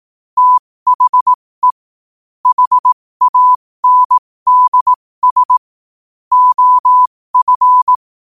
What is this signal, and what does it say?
Morse code, heard as a keyed tone, about 18 wpm, spaced as for 15 wpm, 1000 Hz. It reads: THE HANDS OF
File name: qotdfiles / 15wpm